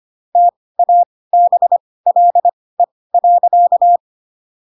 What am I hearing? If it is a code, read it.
Morse code, heard as a keyed tone, about 25 wpm, spaced as for 18 wpm, 700 Hz. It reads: TABLE.
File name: Pinocchio / Pinocchio ch7 12 25 ik1hge